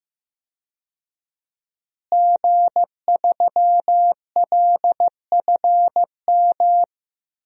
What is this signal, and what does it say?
Morse code, heard as a keyed tone, about 16 wpm, 700 Hz. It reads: G3LFM
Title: DXCC-100 Call Signs - 5 Characters - Repeated 3x - 15wpm